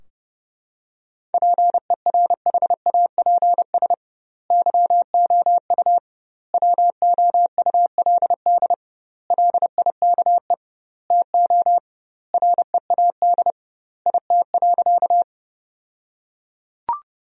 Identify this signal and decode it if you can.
Morse code, heard as a keyed tone, about 30 wpm, 700 Hz. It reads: PERHAPS YOU WOULD LIKE TO READ IT.
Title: 2x Word-Spacing - Sentences from Top 500 Words - 30wpm